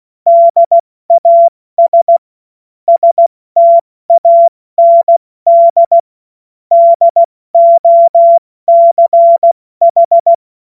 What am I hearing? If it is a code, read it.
Morse code, heard as a keyed tone, about 16 wpm, 680 Hz. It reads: DAS STAND DOCH
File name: Morsekurs / Witze / Witz 07.November 2023 70 F80